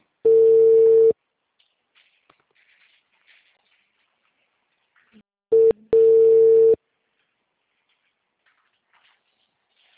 Ringback tone (China, 450 Hz).
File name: out- -2037- - - .25501